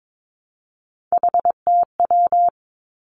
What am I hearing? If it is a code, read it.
Morse code, heard as a keyed tone, about 22 wpm, 700 Hz. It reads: HTW
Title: Three Challenging Letter-Number Combinations 22wpm